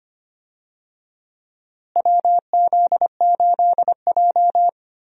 Morse code, heard as a keyed tone, about 25 wpm, 700 Hz. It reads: WZ8J
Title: USA Call Signs - 4 Characters - Repeated 2x - 25wpm